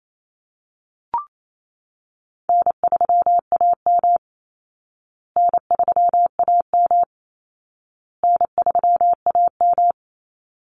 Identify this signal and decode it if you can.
Morse code, heard as a keyed tone, about 28 wpm, 700 Hz. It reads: N3AM N3AM N3AM
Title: Easy Call Signs Repeated 3x - 28wpm